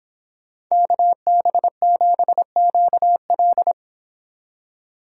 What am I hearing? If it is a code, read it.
Morse code, heard as a keyed tone, about 26 wpm, 700 Hz. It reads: KB7QL